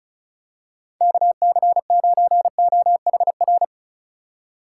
Morse code, heard as a keyed tone, about 34 wpm, 700 Hz. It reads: KC9OHR
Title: USA Call Signs - 6 Characters - 4x Character Spacing - 35wpm